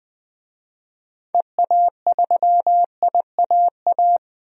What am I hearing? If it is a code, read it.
Morse code, heard as a keyed tone, about 20 wpm, 700 Hz. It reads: EA3IAA
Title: DXCC-100 Call Signs - 6 Characters - 20wpm